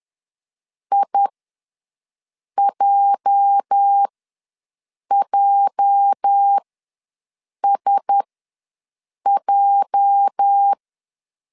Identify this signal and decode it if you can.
Morse code, heard as a keyed tone, about 10 wpm, spaced as for 6 wpm, 800 Hz. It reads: IJJSJ
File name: LEZIONI / 8 YJ45 / 8 YJ ws 10wpm 4043